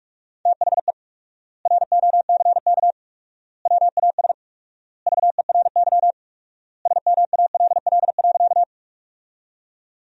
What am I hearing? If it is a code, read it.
Morse code, heard as a keyed tone, about 46 wpm, 700 Hz. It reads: THE ROCK WAS VERY SMALL.
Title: 4x Word-Spacing - Sentences from Top 400 Words - 45wpm